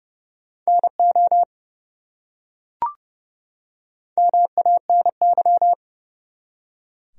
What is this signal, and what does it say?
Morse code, heard as a keyed tone, about 30 wpm, 700 Hz. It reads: NO MANY
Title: Top 100 Words 30wpm